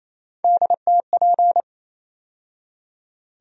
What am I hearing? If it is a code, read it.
Morse code, heard as a keyed tone, about 28 wpm, 700 Hz. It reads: DTP
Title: USA Call Sign Postfixes - 3 Letters 28wpm